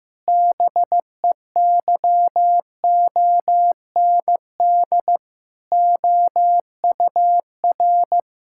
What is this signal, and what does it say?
Morse code, heard as a keyed tone, about 15 wpm, 700 Hz. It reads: BEYOND OUR
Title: Sentences from Top 1000 Words - Review Part 6 - 15wpm